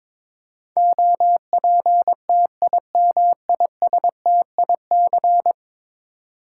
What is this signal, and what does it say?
Morse code, heard as a keyed tone, about 22 wpm, 700 Hz. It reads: OPTIMISTIC
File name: practice / 22wpm / 10 Letter Words - First 3 Letters Given - 22wpm